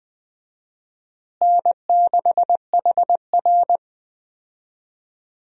Morse code, heard as a keyed tone, about 20 wpm, 700 Hz. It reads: N6HR
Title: USA Call Signs - 4 Characters - Repeated 2x - 20wpm